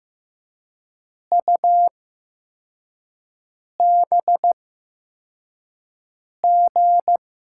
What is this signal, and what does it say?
Morse code, heard as a keyed tone, about 15 wpm, spaced as for 4 wpm, 700 Hz. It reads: UBG
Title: USA Call Signs - 8x Character Spacing - 15wpm